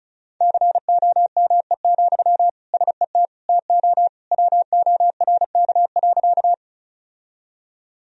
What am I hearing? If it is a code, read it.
Morse code, heard as a keyed tone, about 34 wpm, 700 Hz. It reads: COME, SET TO WORK.
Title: Sentences from Top 1000 Words - Review Part 1 - 35wpm